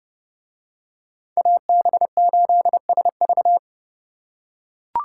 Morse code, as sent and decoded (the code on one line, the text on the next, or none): .- -... ---.. ... ...-
AB8SV